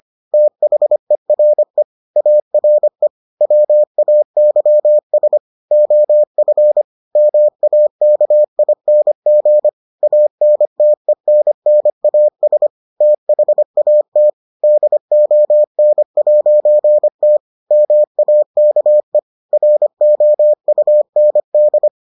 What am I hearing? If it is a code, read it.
Morse code, heard as a keyed tone, about 25 wpm, 600 Hz. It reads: THERE ARE WAYS OF MAKING ANTENNAS THAT DON'T MAKE ROUND